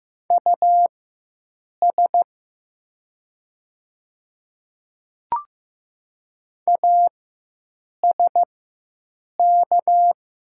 Morse code, heard as a keyed tone, about 15 wpm, spaced as for 7 wpm, 700 Hz. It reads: US ASK
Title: 4x Character Spacing - Top 300 Words - 15wpm